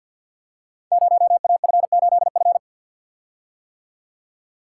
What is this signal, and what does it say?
Morse code, heard as a keyed tone, about 48 wpm, 700 Hz. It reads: 0AF8F
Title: Code Groups - 5 Letter:Numbers - 50wpm